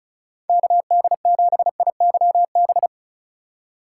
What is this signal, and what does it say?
Morse code, heard as a keyed tone, about 36 wpm, 690 Hz. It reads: KD7IYB
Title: USA Call Signs - 3x Character Spacing - 35wpm